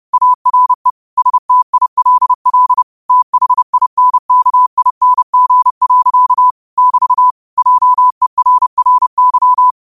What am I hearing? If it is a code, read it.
Morse code, heard as a keyed tone, about 30 wpm, 1000 Hz. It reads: ARE STILL THINKING. <BT> JERRY
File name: 30wpm